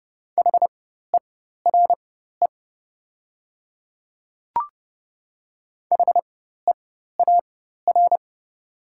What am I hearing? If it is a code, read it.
Morse code, heard as a keyed tone, about 30 wpm, spaced as for 14 wpm, 700 Hz. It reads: HERE HEAR
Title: 4x Character Spacing - Top 300 Words - 30wpm